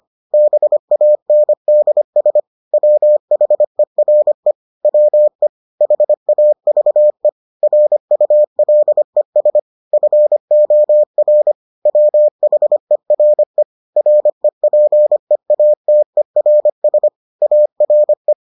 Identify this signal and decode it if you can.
Morse code, heard as a keyed tone, about 25 wpm, 600 Hz. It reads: BANDS WHERE WE HAVE RULES FOR WHERE REPEATERS ARE